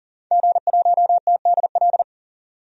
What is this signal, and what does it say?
Morse code, heard as a keyed tone, about 39 wpm, 700 Hz. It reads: G1TDL
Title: DXCC-100 Call Signs - 5 Characters - 40wpm